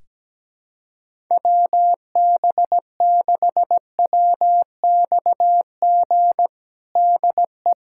Morse code, heard as a keyed tone, about 17 wpm, 700 Hz. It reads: WB6WXG DE
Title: USA Callsign DE Callsign 17wpm